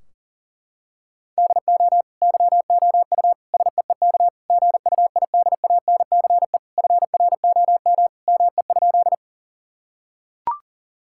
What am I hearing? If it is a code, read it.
Morse code, heard as a keyed tone, about 40 wpm, 700 Hz. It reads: DO YOU SEEK GUIDANCE FROM ME?
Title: Sentences from Top 1600 Words 40wpm